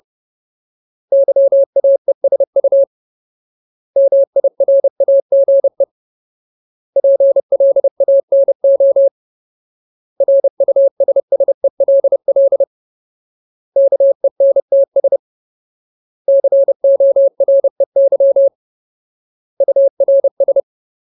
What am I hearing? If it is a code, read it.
Morse code, heard as a keyed tone, about 30 wpm, 550 Hz. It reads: YAESU MIRAGE PLANO RUSSELL KENTS COREY URS
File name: Ultimate Word List / 30WPM Ultimate